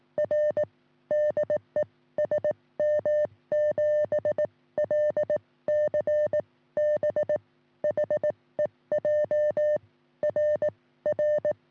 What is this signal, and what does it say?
Morse code, heard as a keyed tone, about 18 wpm, spaced as for 15 wpm, 590 Hz. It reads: R DE SM7LCB HEJ RR
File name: sm7lcb / 1941 sm0err 10g rs